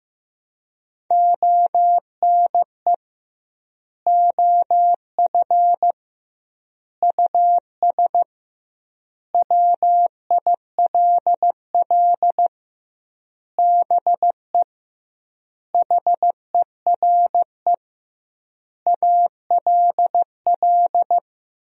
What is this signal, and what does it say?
Morse code, heard as a keyed tone, about 16 wpm, 700 Hz. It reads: ONE OF US WILL BE HERE ALL
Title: 2x Word-Spacing - Sentences from Top 300 Words - 15wpm